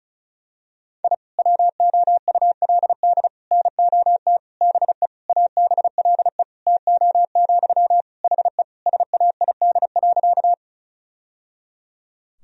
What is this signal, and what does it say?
Morse code, heard as a keyed tone, about 36 wpm, 700 Hz. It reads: I WOULD NOT BE ABLE TO, HE SAID.
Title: Sentences from Top 1000 Words - Review Part 2 - 35wpm